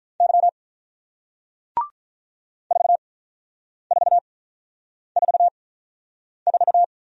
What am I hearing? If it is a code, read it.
Morse code, heard as a keyed tone, about 44 wpm, spaced as for 9 wpm, 700 Hz. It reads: Y 4444
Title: Speed-Racing Single Letter-Number 35wpm